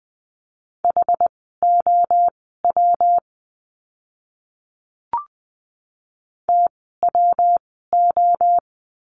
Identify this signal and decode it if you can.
Morse code, heard as a keyed tone, about 20 wpm, spaced as for 15 wpm, 700 Hz. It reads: HOW TWO
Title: 146 Lesson 24f - Letter y - Words 20wpm